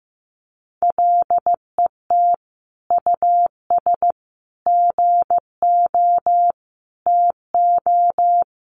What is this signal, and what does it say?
Morse code, heard as a keyed tone, about 15 wpm, 700 Hz. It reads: LET US GO TO
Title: Sentences from Top 1100 Words 15wpm